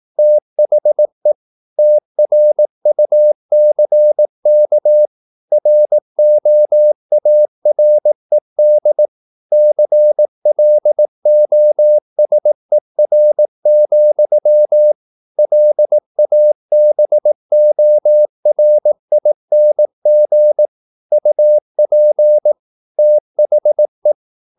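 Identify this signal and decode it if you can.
Morse code, heard as a keyed tone, about 18 wpm, 600 Hz. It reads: THE TRUCK ROARED CLOSER, LABORING UP THE
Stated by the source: Morse practice at 18 wpm, English text, from CW med Gnister.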